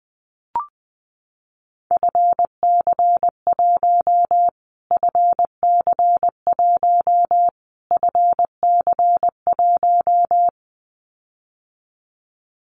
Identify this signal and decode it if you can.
Morse code, heard as a keyed tone, about 20 wpm, 700 Hz. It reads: FC1 FC1 FC1
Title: Callsign Prefixes for Top 25 DXCC - Repeated 3x - 20wpm